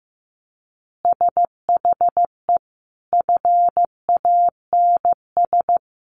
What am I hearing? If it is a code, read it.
Morse code, heard as a keyed tone, about 15 wpm, 700 Hz. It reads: SHE FANS
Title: Additional Easy Phrases 15wpm